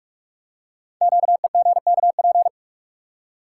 Morse code, heard as a keyed tone, about 44 wpm, 700 Hz. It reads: QEGKP
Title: Code Groups - 5 Letters - 45wpm